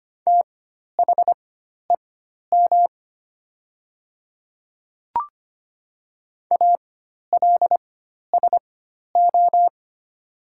Morse code, heard as a keyed tone, about 25 wpm, spaced as for 12 wpm, 700 Hz. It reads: THEM ALSO
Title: 4x Character Spacing - Top 300 Words - 25wpm